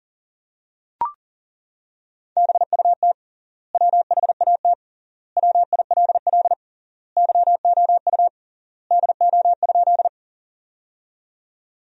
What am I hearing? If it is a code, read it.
Morse code, heard as a keyed tone, about 40 wpm, 700 Hz. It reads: BUT WHAT WILL YOU DO?
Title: 3x Word-Spacing - Sentences from Top 100 Words - 40wpm